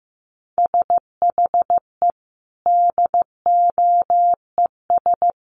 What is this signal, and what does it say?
Morse code, heard as a keyed tone, about 15 wpm, 700 Hz. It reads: SHE DOES
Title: Sentences from Top 1300 Words 15wpm